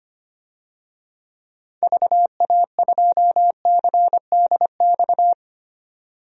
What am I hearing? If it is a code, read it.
Morse code, heard as a keyed tone, about 25 wpm, 700 Hz. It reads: VA2CDX
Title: DXCC-100 Call Signs - 6 Characters - 25wpm